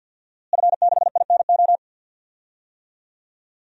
Morse code, heard as a keyed tone, about 52 wpm, 700 Hz. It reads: F6INO
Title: DXCC-100 Call Signs - 5 Characters - Repeated 2x - 50wpm